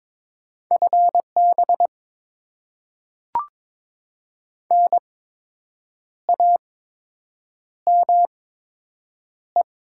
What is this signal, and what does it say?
Morse code, heard as a keyed tone, about 22 wpm, spaced as for 7 wpm, 700 Hz. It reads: FB NAME
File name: Top 20 QSO Elements - 8x Character Spacing - 22wpm